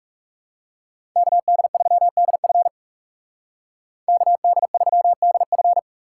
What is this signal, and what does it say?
Morse code, heard as a keyed tone, about 38 wpm, 700 Hz. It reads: KD3DF KD3DF